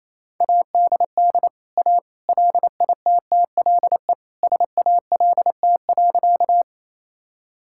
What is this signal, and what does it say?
Morse code, heard as a keyed tone, about 28 wpm, 700 Hz. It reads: ADD A LITTLE SALT.
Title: Sentences from Top 1000 Words - Review Part 5 - 28wpm